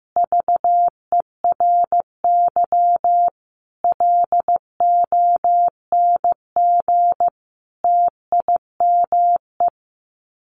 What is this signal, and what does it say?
Morse code, heard as a keyed tone, about 15 wpm, 700 Hz. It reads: VERY LONG TIME
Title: Sets of 5 Words 15wpm